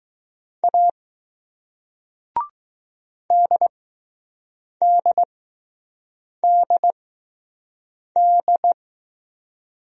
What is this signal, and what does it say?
Morse code, heard as a keyed tone, about 20 wpm, 700 Hz. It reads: A D D D D